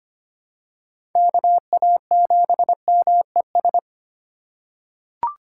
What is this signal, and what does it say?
Morse code, heard as a keyed tone, about 25 wpm, 700 Hz. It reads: KA7MES
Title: USA Call Signs - 6 Characters - 6x Character Spacing - 25wpm